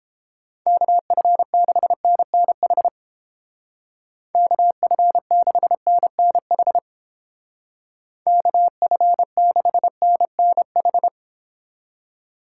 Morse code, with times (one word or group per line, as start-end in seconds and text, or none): 0.7-2.9 KF6NNH
4.3-6.8 KF6NNH
8.3-11.1 KF6NNH